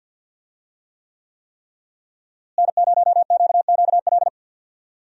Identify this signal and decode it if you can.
Morse code, heard as a keyed tone, about 50 wpm, 700 Hz. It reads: N0QQL
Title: USA Call Signs - 5 Characters - Repeated 3x - 50wpm